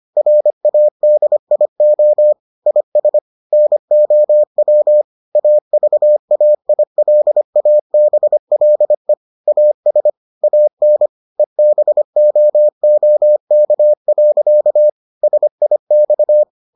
Morse code, heard as a keyed tone, about 25 wpm, 600 Hz. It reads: RADIO IS NOW AVAILABLE AS AN EBOOK. SIX